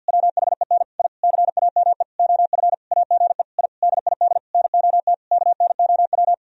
Morse code, heard as a keyed tone, about 50 wpm, 700 Hz. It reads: WHEN I CAME OF AGE I DID NOT KNOW
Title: Sentences from Top 1000 Words - Review Part 3 - 50wpm